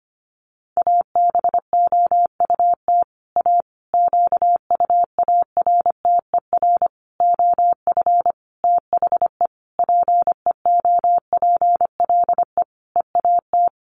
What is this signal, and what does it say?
Morse code, heard as a keyed tone, about 25 wpm, 700 Hz. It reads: ABOUT A QUARTER OF THE PEOPLE EAT
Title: Sentences from Top 1300 Words 25wpm